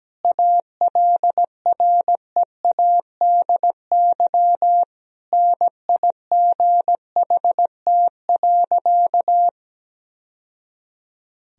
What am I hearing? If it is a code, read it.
Morse code, heard as a keyed tone, about 17 wpm, 700 Hz. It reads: ALREADY NIGHT.
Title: Sentences from Top 1000 Words - Review Part 1 - 17wpm